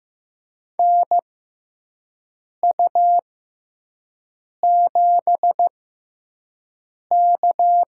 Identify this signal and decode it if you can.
Morse code, heard as a keyed tone, about 15 wpm, spaced as for 5 wpm, 700 Hz. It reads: NU7K
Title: USA Call Signs - 4 Characters - 6x Character Spacing - 15wpm